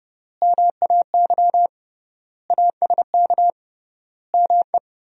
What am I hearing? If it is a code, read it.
Morse code, heard as a keyed tone, about 30 wpm, 700 Hz. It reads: MAY ASK ME
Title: 3x Word-Spacing - Sentences from Top 200 Words - 30wpm